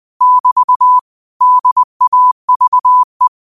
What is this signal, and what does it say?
Morse code, heard as a keyed tone, about 20 wpm, 1000 Hz. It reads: <BT> DAVE
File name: qotdfiles / 20wpm